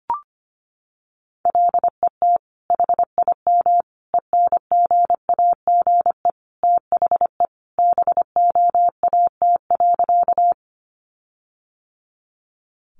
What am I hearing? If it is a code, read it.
Morse code, heard as a keyed tone, about 25 wpm, 700 Hz. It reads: LET HIM ENGAGE THE BOAT.
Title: Sentences from Top 1500 Words 25wpm